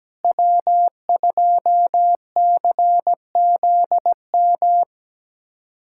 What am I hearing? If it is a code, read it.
Morse code, heard as a keyed tone, about 17 wpm, 700 Hz. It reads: W2CZM
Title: USA Callsign DE Callsign 17wpm